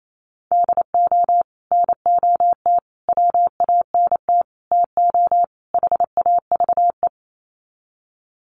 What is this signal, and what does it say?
Morse code, heard as a keyed tone, about 28 wpm, 700 Hz. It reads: DO NOT WANT TO HAVE
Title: Sets of 5 Words 28wpm